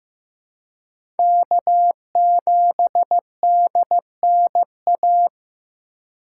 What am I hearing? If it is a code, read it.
Morse code, heard as a keyed tone, about 15 wpm, 700 Hz. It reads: K7DNA
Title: USA Call Signs - 5 Characters - 4x Character Spacing - 15wpm